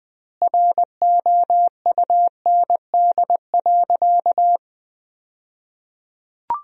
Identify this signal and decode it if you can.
Morse code, heard as a keyed tone, about 20 wpm, 700 Hz. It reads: ROUND.
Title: 2x Word-Spacing - Sentences from Top 200 Words - 20wpm